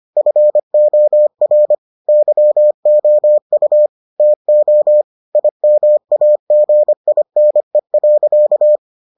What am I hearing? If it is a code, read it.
Morse code, heard as a keyed tone, about 25 wpm, 600 Hz. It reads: FOR YOU TO IMAGINE.